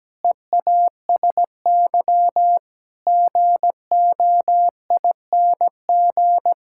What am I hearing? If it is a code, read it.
Morse code, heard as a keyed tone, about 17 wpm, 700 Hz. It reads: EASY GOING
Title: Sentences from Top 1000 Words - Review Part 5 - 17wpm